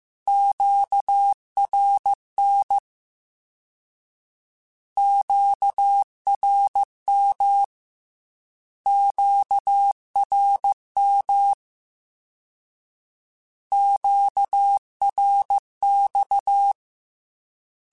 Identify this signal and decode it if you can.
Morse code, heard as a keyed tone, about 15 wpm, 800 Hz. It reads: QRN QRM QRM QRX